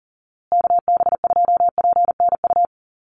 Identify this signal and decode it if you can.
Morse code, heard as a keyed tone, about 40 wpm, 700 Hz. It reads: KB2PNU
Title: USA Call Signs Repeated 2x 40wpm